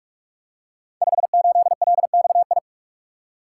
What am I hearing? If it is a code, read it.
Morse code, heard as a keyed tone, about 44 wpm, 700 Hz. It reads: 58LXI
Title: Code Groups - 5 Letter:Numbers - 45wpm